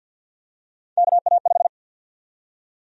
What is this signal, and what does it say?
Morse code, heard as a keyed tone, about 47 wpm, 700 Hz. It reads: KA5
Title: USA Call Sign Prefixes - Repeated 2x - 50wpm